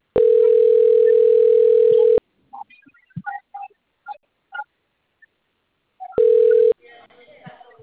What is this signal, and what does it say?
Ringback tone (North America), 440 + 480 Hz, with speech.
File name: out- - - - - .70447